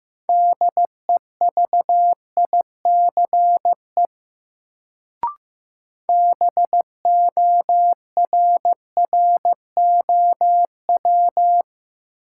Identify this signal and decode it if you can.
Morse code, heard as a keyed tone, about 15 wpm, 700 Hz. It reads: DEVICE BORROW
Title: Sentences from Top 1500 Words - New Word Drill Alt - 15wpm